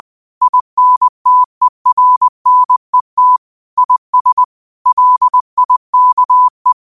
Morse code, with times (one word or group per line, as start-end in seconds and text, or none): 0.4-3.4 INTERNET
3.8-4.4 IS
4.9-6.7 LIKE